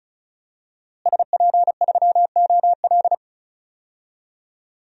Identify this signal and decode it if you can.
Morse code, heard as a keyed tone, about 34 wpm, 700 Hz. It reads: SP3OL
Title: DXCC-100 Call Signs - 5 Characters - Repeated 2x - 35wpm